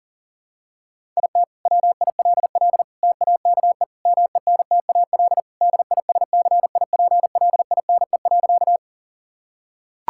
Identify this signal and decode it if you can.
Morse code, heard as a keyed tone, about 40 wpm, 700 Hz. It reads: IT WILL TAKE MENTAL DISCIPLINE.
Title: Sentences from Top 1000 Words - Review Part 6 - 40wpm